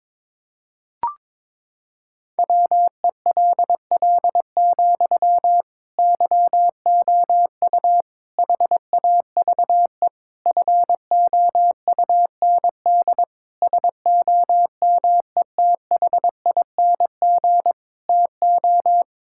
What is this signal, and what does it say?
Morse code, heard as a keyed tone, about 22 wpm, 700 Hz. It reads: WELL, YOU HAVE FOUND SOMETHING TO
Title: Sentences from Top 1000 Words - Review Part 6 - 22wpm